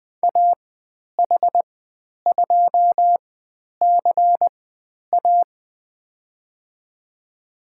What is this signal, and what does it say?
Morse code, heard as a keyed tone, about 20 wpm, spaced as for 10 wpm, 700 Hz. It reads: AH2CA
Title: 102 Lesson 17g - Letter u - Call Signs 20wpm